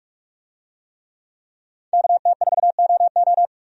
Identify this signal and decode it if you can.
Morse code, heard as a keyed tone, about 46 wpm, 700 Hz. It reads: KT4OO
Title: USA Call Signs - 5 Characters - 45wpm